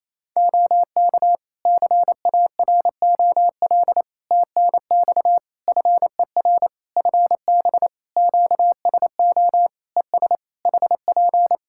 Morse code, heard as a keyed tone, about 28 wpm, 700 Hz. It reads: OK CAROL TNX FER FB QSO ES HP
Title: K1USN SST QSO Practice 28wpm